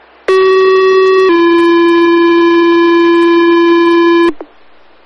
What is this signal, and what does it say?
Two-tone page, heard as a one-second tone, then a three-second tone, about 380 then 340 Hz, with noise. It is heard from 0.3 to 4.3 s.